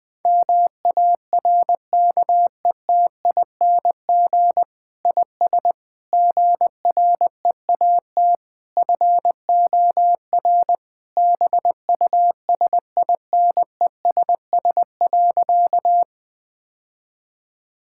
Morse code, heard as a keyed tone, about 21 wpm, 700 Hz. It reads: MARKETING IS GREAT FOR BUSINESS.
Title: Sentences from Top 1000 Words - Review Part 5 - 20wpm